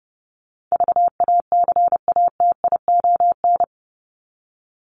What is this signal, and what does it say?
Morse code, heard as a keyed tone, about 30 wpm, 700 Hz. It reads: VACATION
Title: Top 1501-1750 Words 30wpm